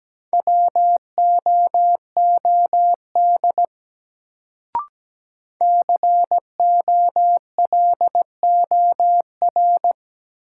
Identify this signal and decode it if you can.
Morse code, heard as a keyed tone, about 17 wpm, 700 Hz. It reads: WOOD COLOR
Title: Sentences from Top 1000 Words - Review Part 2 - New Word Drill - 17wpm